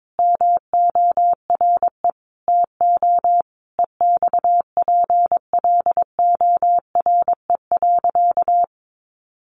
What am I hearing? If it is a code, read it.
Morse code, heard as a keyed tone, about 22 wpm, 700 Hz. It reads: MORE TO EXPLORE.
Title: Sentences from Top 1100 Words 22wpm